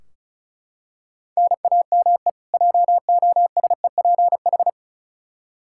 Morse code, heard as a keyed tone, about 36 wpm, 700 Hz. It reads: NAME JOSEPH
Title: Practice QSO - Exchange 1 - Beginning - 35wpm